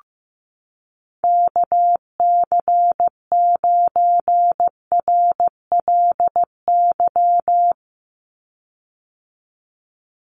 Morse code, heard as a keyed tone, about 15 wpm, 700 Hz. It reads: KC9RLY
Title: USA Call Signs 15wpm